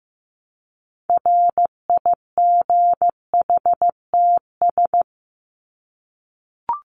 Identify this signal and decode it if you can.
Morse code, heard as a keyed tone, about 15 wpm, 700 Hz. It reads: RIGHTS
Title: ES:S Suffix - Top 100 Words with Base Words 15wpm